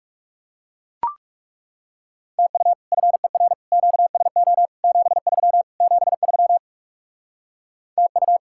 Morse code, heard as a keyed tone, about 38 wpm, 700 Hz. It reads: TU FER QSO 73 73 TU